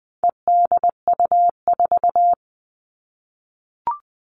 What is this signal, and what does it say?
Morse code, heard as a keyed tone, about 20 wpm, 700 Hz. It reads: EDU4
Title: All Cut Numbers - 4-Digit 20wpm